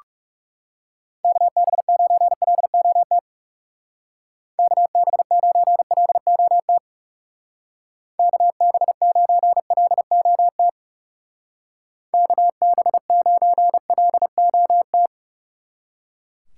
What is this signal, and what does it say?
Morse code, heard as a keyed tone, about 36 wpm, 700 Hz. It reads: KB9LOT KB9LOT KB9LOT KB9LOT
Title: Speed Racing - USA Call Signs 30wpm